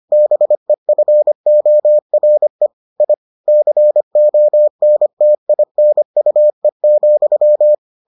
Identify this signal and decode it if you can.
Morse code, heard as a keyed tone, about 25 wpm, 600 Hz. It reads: BEFORE I CONTINUE,